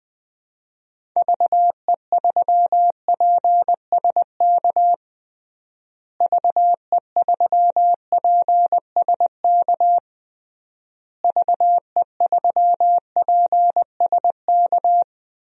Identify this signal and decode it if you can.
Morse code, heard as a keyed tone, about 20 wpm, 700 Hz. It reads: VE3PSK VE3PSK VE3PSK
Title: DXCC-100 Call Signs - 6 Characters - Repeated 3x - 20wpm